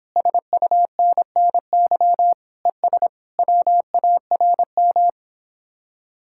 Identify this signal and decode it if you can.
Morse code, heard as a keyed tone, about 26 wpm, 700 Hz. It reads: SUNNY ES WARM